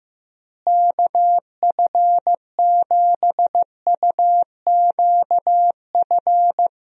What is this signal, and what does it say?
Morse code, heard as a keyed tone, about 16 wpm, 700 Hz. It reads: KF7UQF
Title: USA Call Signs - 6 Characters - Repeated 2x - 15wpm